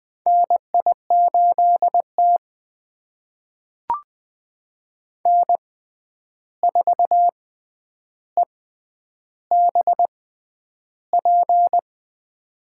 Morse code, heard as a keyed tone, about 20 wpm, spaced as for 10 wpm, 700 Hz. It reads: NI8T N4EBP
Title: USA Call Signs - 6x Character Spacing - 20wpm